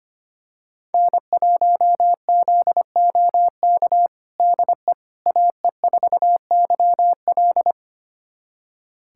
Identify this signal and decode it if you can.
Morse code, heard as a keyed tone, about 25 wpm, 700 Hz. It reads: N1ZOK DE AE4YL
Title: USA Callsign DE Callsign - 5 Characters - 25wpm